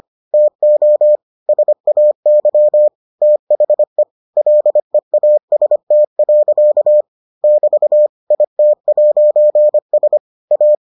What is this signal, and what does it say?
Morse code, heard as a keyed tone, about 25 wpm, 600 Hz. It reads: TO SAY THE LEAST. <BT> IT'S A